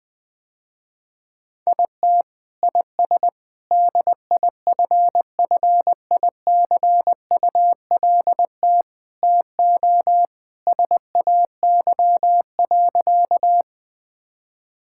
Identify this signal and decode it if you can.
Morse code, heard as a keyed tone, about 20 wpm, 700 Hz. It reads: IT IS DIFFICULT TO SAY.
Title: Sentences with 4-Word Phrases - Spoken Beginning - 20wpm